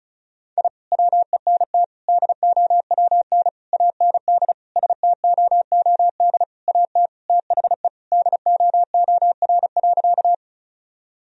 Morse code, heard as a keyed tone, about 36 wpm, 700 Hz. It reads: I WENT DOWN AND STOOD AT THE DOOR.
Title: Sentences from Top 1000 Words - Review Part 2 - 35wpm